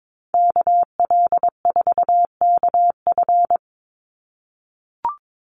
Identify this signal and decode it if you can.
Morse code, heard as a keyed tone, about 22 wpm, 700 Hz. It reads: KL4KF
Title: USA Call Signs 22wpm